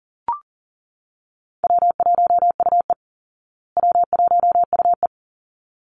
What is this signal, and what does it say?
Morse code, heard as a keyed tone, about 40 wpm, 700 Hz. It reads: W1UE W1UE
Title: Call Signs Repeated 3x - 40wpm